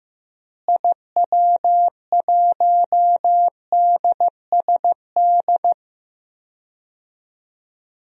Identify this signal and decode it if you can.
Morse code, heard as a keyed tone, about 15 wpm, 700 Hz. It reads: IW1DSD
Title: DXCC-100 Call Signs - 6 Characters - Repeated 2x - 15wpm